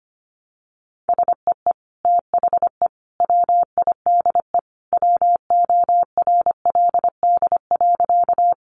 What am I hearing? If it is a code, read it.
Morse code, heard as a keyed tone, about 25 wpm, 700 Hz. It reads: SEE THE WIDE WORLD.
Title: Sentences from Top 900 Words 25wpm